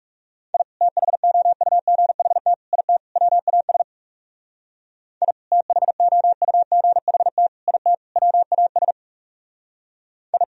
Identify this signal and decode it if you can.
Morse code, heard as a keyed tone, about 35 wpm, 700 Hz. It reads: I THOUGHT IT WAS I THOUGHT IT WAS I
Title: Speed-Racing - Sets of 4 Words 30wpm